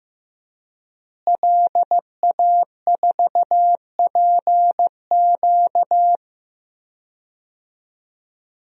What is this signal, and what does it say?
Morse code, heard as a keyed tone, about 15 wpm, 700 Hz. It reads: LA4PQ
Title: DXCC-100 Call Signs - 5 Characters - Repeated 2x - 15wpm